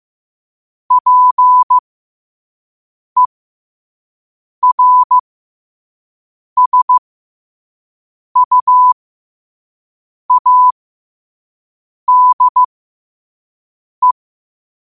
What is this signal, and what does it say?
Morse code, heard as a keyed tone, about 15 wpm, spaced as for 5 wpm, 1000 Hz. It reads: PERSUADE
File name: qotdfiles / 5wpm